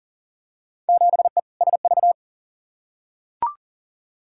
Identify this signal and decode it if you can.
Morse code, heard as a keyed tone, about 39 wpm, 700 Hz. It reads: 7E SV
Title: Field Day Exchanges Encore - 40wpm